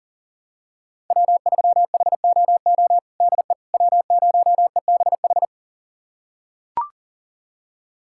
Morse code, heard as a keyed tone, about 40 wpm, 700 Hz. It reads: W3HOO DE W0EBH
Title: USA Callsign DE Callsign - 5 Characters - 40wpm